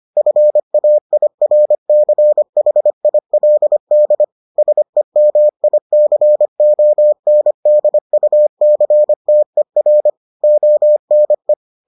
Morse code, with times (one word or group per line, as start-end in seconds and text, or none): 0.2-4.2 FAIRCHILD
4.6-10.1 SEMICONDUCTER
10.4-11.5 ONE